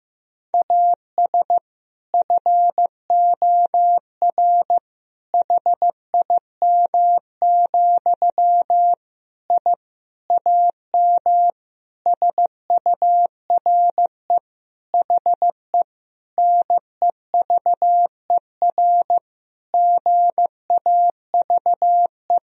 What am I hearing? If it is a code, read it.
Morse code, heard as a keyed tone, about 15 wpm, 700 Hz. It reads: AS FOR HIM, I AM SURE HE NEVER GAVE
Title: Sentences from Top 1000 Words - Review Part 2 - 15wpm